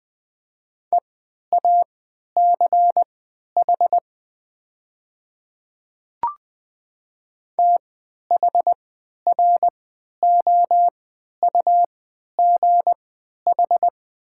Morse code, heard as a keyed tone, about 20 wpm, spaced as for 11 wpm, 700 Hz. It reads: EACH THROUGH
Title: 3x Character Spacing - Top 200 Words - 20wpm